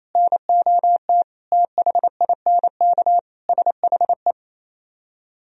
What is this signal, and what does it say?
Morse code, heard as a keyed tone, about 36 wpm, 700 Hz. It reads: NOT THINK SHE